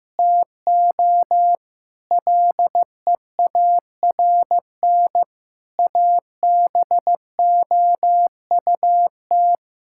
Morse code, heard as a keyed tone, about 15 wpm, 700 Hz. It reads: TO LEARN ABOUT
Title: Sentences from Top 1000 Words - Review Part 6 - 15wpm